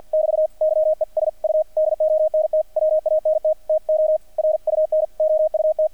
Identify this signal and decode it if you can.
Morse code, heard as a keyed tone, about 50 wpm, 620 Hz, with radio noise. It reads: <BT> YES U DONT WANT TO RUN OUT
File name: archived / CRSnet